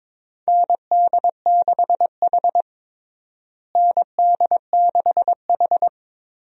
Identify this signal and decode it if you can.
Morse code, heard as a keyed tone, about 22 wpm, 700 Hz. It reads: ND6H ND6H
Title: USA Call Signs - 4 Characters - Repeated 2x - 22wpm